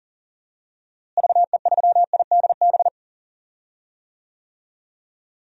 Morse code, heard as a keyed tone, about 40 wpm, 700 Hz. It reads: VE3IDB
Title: DXCC-100 Call Signs - 6 Characters - Repeated 2x - 40wpm